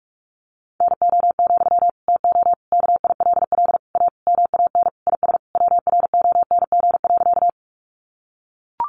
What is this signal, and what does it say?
Morse code, heard as a keyed tone, about 44 wpm, 700 Hz. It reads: NO, TO KILL A MAN IS WRONG.
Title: Sentences from Top 1400 Words 45wpm